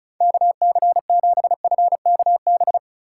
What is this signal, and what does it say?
Morse code, heard as a keyed tone, about 43 wpm, 700 Hz. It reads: KC7FKB
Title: Speed Racing - USA Call Signs 35wpm